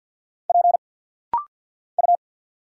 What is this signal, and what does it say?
Morse code, heard as a keyed tone, about 50 wpm, 700 Hz. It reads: P U
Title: Single Letters - Rapid-Fire 50wpm